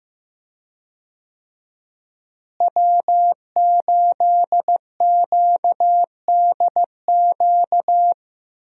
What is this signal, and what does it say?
Morse code, heard as a keyed tone, about 16 wpm, 700 Hz. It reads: W8QDQ
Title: USA Call Signs - 5 Characters - Repeated 2x - 15wpm